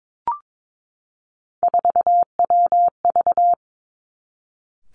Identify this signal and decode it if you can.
Morse code, heard as a keyed tone, about 22 wpm, 700 Hz. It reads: 4WV
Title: Three Challenging Letter-Number Combinations 22wpm